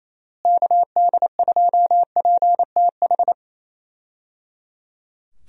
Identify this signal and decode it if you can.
Morse code, heard as a keyed tone, about 27 wpm, 700 Hz. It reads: KD2PTH
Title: USA Call Signs - 6 Characters - Repeated 3x - 28wpm